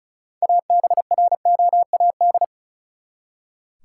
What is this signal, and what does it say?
Morse code, heard as a keyed tone, about 36 wpm, 700 Hz. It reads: ABROAD
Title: Six Letter Words - 35wpm